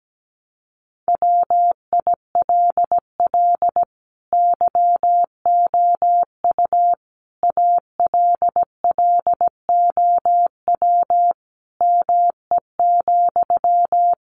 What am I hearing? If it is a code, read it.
Morse code, heard as a keyed tone, about 17 wpm, 700 Hz. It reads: WILL YOU ALLOW ME,
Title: Sentences from Top 700 Words 17wpm